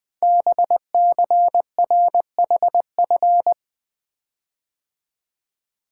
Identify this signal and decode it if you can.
Morse code, heard as a keyed tone, about 20 wpm, 700 Hz. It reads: BCRHF
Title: Code Groups - 5 Letters - 20wpm